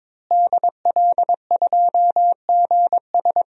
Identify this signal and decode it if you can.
Morse code, heard as a keyed tone, about 22 wpm, 700 Hz. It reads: DL2GS
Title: DXCC-100 Call Signs - 5 Characters - Repeated 3x - 22wpm